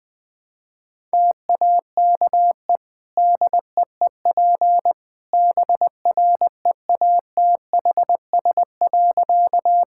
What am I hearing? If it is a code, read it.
Morse code, heard as a keyed tone, about 20 wpm, 700 Hz. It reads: TAKE DEEP BREATHS.
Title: Easy Sentences with Hints - 20wpm